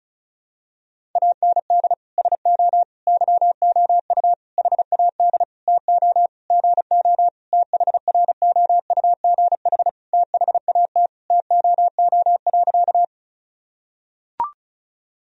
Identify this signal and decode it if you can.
Morse code, heard as a keyed tone, about 34 wpm, 700 Hz. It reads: AND SO YOU HAD TO GO THROUGH THAT TOO.
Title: Sentences from Top 1000 Words - Review Part 1 - 35wpm